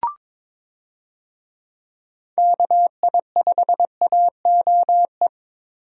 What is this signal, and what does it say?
Morse code, heard as a keyed tone, about 23 wpm, 700 Hz. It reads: KI5AOE
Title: USA Call Signs - 6 Characters - 22wpm